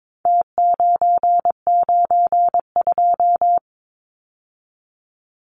Morse code, heard as a keyed tone, about 22 wpm, 700 Hz. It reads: T992
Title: Numbers - Contest Style - 4-Digit 22wpm